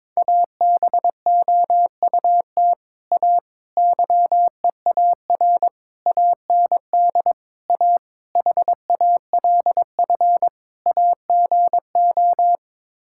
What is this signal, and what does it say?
Morse code, heard as a keyed tone, about 22 wpm, 700 Hz. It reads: ABOUT A YEAR AND A HALF AGO
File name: practice / 22wpm / Sets of 7 Words - 22wpm